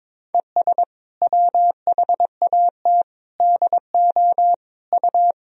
Morse code, heard as a keyed tone, about 22 wpm, 700 Hz. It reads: ES WHAT DO U
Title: Sentences from Top 100 Words and Abbreviations 22wpm